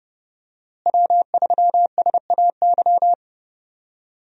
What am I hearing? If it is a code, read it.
Morse code, heard as a keyed tone, about 30 wpm, 700 Hz. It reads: W3SAY
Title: USA Call Signs - 5 Characters - Repeated 2x - 30wpm